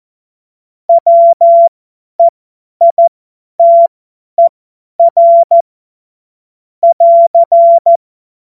Morse code, heard as a keyed tone, about 14 wpm, spaced as for 10 wpm, 680 Hz. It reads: WEITER <AR>
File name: Morsekurs / Witze / Witz 17 Juni 2025 50 F70